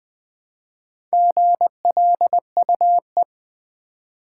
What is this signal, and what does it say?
Morse code, heard as a keyed tone, about 20 wpm, 700 Hz. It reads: GLUE
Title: Easy Words with Hints - 20wpm